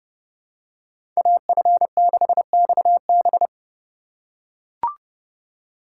Morse code, heard as a keyed tone, about 32 wpm, 700 Hz. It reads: AF6XB
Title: USA Call Signs - 5 Characters - 4x Character Spacing - 30wpm